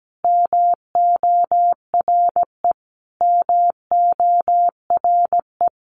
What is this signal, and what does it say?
Morse code, heard as a keyed tone, about 17 wpm, 700 Hz. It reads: MORE MORE
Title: Top 200 Words Repeated 17wpm